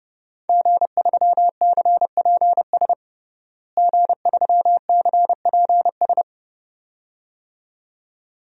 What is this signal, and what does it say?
Morse code, heard as a keyed tone, about 30 wpm, 700 Hz. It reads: G3CPS G3CPS
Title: DXCC-100 Call Signs - 5 Characters - Repeated 2x - 30wpm